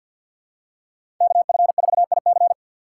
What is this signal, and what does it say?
Morse code, heard as a keyed tone, about 53 wpm, 690 Hz. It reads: KF4IC
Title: USA Call Signs - 5 Characters - 3x Character Spacing - 50wpm